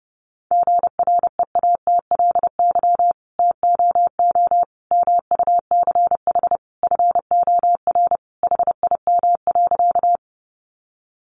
Morse code, heard as a keyed tone, about 30 wpm, 700 Hz. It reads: GREATLY TOO MUCH FOR HIM.
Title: Sentences from Top 1400 Words 30wpm